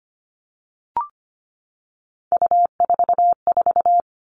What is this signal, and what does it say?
Morse code, heard as a keyed tone, about 25 wpm, 700 Hz. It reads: U44
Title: All Cut Numbers - 3-Digit 25wpm